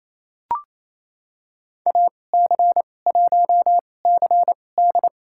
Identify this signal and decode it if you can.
Morse code, heard as a keyed tone, about 28 wpm, spaced as for 20 wpm, 700 Hz. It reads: AC1CD
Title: USA Call Signs - 5 Characters - 2x Character Spacing - 28wpm